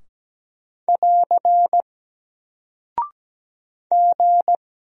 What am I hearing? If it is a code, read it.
Morse code, heard as a keyed tone, about 17 wpm, 700 Hz. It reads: <AR> G
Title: Single Character (Letter:Number:Punctuation:Prosign) Alt 17wpm